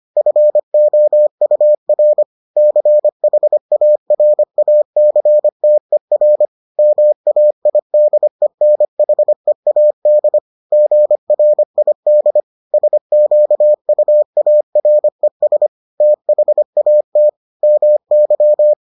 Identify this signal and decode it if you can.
Morse code, heard as a keyed tone, about 25 wpm, 600 Hz. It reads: FOUR CHARACTER MAIDENHEAD GRID SQUARES THAT MY